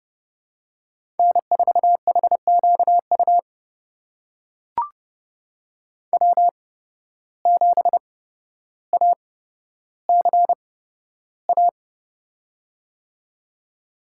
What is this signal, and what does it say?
Morse code, heard as a keyed tone, about 30 wpm, spaced as for 13 wpm, 700 Hz. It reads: N4HQU W7ACA
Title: USA Call Signs - 5 Characters - 8x Character Spacing - 30wpm